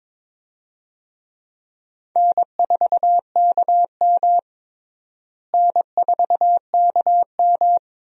Morse code, heard as a keyed tone, about 22 wpm, 700 Hz. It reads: N4KM N4KM
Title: USA Call Signs - 4 Characters - Repeated 2x - 22wpm